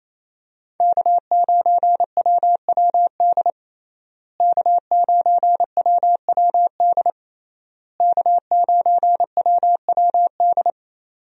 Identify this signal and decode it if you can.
Morse code, heard as a keyed tone, about 28 wpm, 700 Hz. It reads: K9WWD K9WWD K9WWD
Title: USA Call Signs - 5 Characters - Repeated 3x - 28wpm